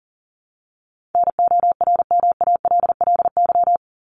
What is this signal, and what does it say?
Morse code, heard as a keyed tone, about 40 wpm, 700 Hz. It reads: NORMALLY
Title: Top 751-1000 Words 40wpm